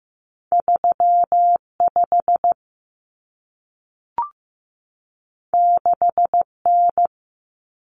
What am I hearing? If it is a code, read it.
Morse code, heard as a keyed tone, about 15 wpm, 700 Hz. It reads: 35 6N
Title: Common Cut Numbers - 2-Digit 15wpm